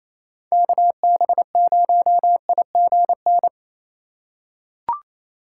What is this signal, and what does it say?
Morse code, heard as a keyed tone, about 28 wpm, 700 Hz. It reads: KB0IGN
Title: USA Call Signs - 6 Characters - 4x Character Spacing - 28wpm